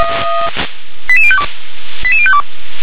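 Five-tone selective call (ZVEI), heard as quick runs of five tones, tones of about 70 ms. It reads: 88052 88052
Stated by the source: ZVEI five-tone selective call received on an SDR.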